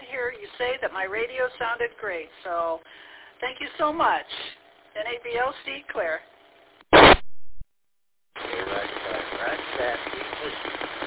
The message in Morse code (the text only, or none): none